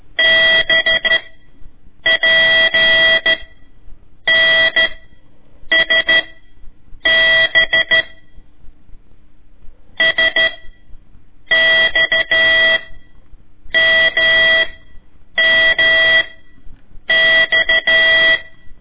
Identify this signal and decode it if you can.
Morse code, heard as a keyed tone, about 13 wpm, 1950 Hz. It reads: B P N S B S X M M X
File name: ik1weg2 / CORSI CW / lez4-1